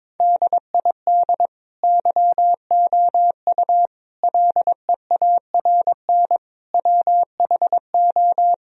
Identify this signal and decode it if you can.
Morse code, heard as a keyed tone, about 22 wpm, 700 Hz. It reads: DID YOU LEARN WHO
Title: Sentences from Top 1000 Words - Review Part 1 - 22wpm